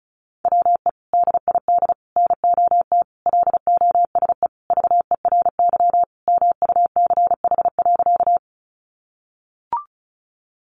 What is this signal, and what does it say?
Morse code, heard as a keyed tone, about 34 wpm, 700 Hz. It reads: WE DID NOT LOSE VERY MUCH.
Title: Sentences from Top 1100 Words 35wpm